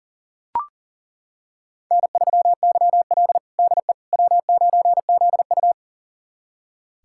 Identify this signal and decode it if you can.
Morse code, heard as a keyed tone, about 40 wpm, 700 Hz. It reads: N3YL DE W9ZU
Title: USA Callsign DE Callsign - 4 Characters - 40wpm